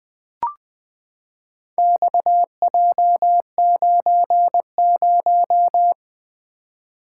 Morse code, heard as a keyed tone, about 20 wpm, 700 Hz. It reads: XJ90
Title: Four Letter-Number Combinations - 20wpm